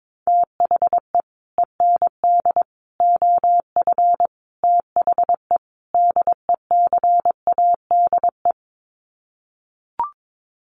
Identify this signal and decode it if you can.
Morse code, heard as a keyed tone, about 22 wpm, 700 Hz. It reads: THE END OF THE DECADE
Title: Sets of 5 Words Encore 22wpm